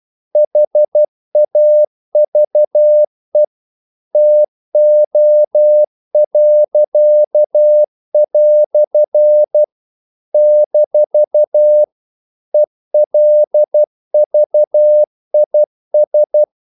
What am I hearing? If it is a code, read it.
Morse code, heard as a keyed tone, about 12 wpm, 600 Hz. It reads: HAVE TO." - ELVIS
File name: CW / Practice / EnglishQuotations0312